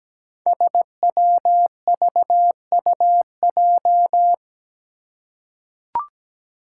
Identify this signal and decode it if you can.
Morse code, heard as a keyed tone, about 17 wpm, 700 Hz. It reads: SWVUJ
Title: Five Challenging Letter-Number Combinations 17wpm